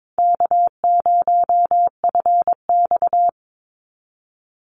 Morse code, heard as a keyed tone, about 22 wpm, 700 Hz. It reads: K0FX
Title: Call Signs - 22wpm